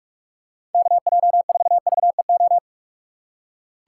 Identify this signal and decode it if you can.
Morse code, heard as a keyed tone, about 46 wpm, 700 Hz. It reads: KJ4VEO
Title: USA Call Signs - 2x Character Spacing - 45wpm